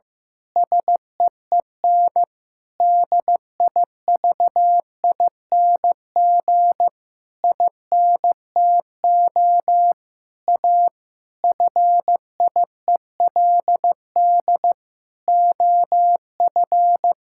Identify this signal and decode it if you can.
Morse code, heard as a keyed tone, about 15 wpm, 700 Hz. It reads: SEEN DIVING INTO A FIELD OF
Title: Pangram Challenge v2 - 15wpm